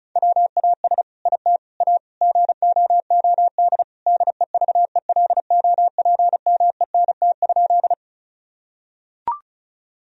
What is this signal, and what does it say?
Morse code, heard as a keyed tone, about 34 wpm, 700 Hz. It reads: WAS IT A GOOD DEVELOPMENT?
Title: Sentences from Top 1000 Words - Review Part 4 - 35wpm